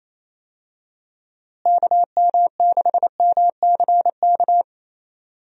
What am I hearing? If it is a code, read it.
Morse code, heard as a keyed tone, about 28 wpm, 700 Hz. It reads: KM6MCK
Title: USA Call Signs - 6 Characters - Repeated 2x - 28wpm